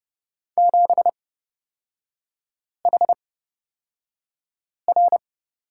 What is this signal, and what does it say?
Morse code, heard as a keyed tone, about 29 wpm, spaced as for 5 wpm, 690 Hz. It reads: 7HR
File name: Lesson 37e - Letter x - Three Characters 30wpm